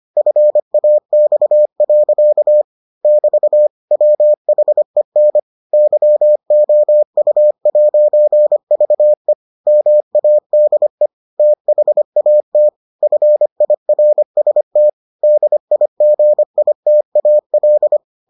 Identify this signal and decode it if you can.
Morse code, heard as a keyed tone, about 25 wpm, 600 Hz. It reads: FAX. <BT> WHEN YOU'VE MADE THAT FIRST DIGITAL